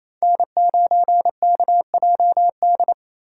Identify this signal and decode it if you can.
Morse code, heard as a keyed tone, about 28 wpm, 700 Hz. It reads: N9KJD
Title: USA Call Signs - 3x Character Spacing - 28wpm